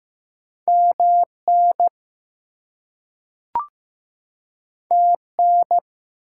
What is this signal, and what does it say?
Morse code, heard as a keyed tone, about 15 wpm, 700 Hz. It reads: MN TN
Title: US States and Canadian Provinces 15wpm